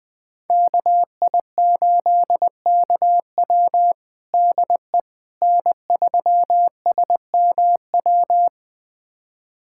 Morse code, heard as a keyed tone, about 21 wpm, 700 Hz. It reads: KI8KW DE N3SMW
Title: USA Callsign DE Callsign - 5 Characters - 20wpm